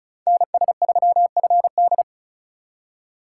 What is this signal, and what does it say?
Morse code, heard as a keyed tone, about 42 wpm, 700 Hz. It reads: NS3FD
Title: Speed Racing - USA 2x2 Call Signs 35wpm